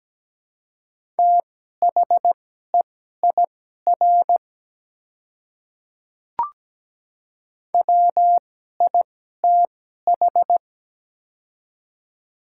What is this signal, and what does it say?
Morse code, heard as a keyed tone, about 17 wpm, spaced as for 12 wpm, 690 Hz. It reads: THEIR WITH
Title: 2x Character Spacing - Top 100 Words - 17wpm